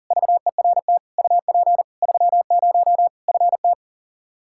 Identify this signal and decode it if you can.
Morse code, heard as a keyed tone, about 40 wpm, 700 Hz. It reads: VERT UP 30 FT
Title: Practice QSO - Exchange 2 - Info - 40wpm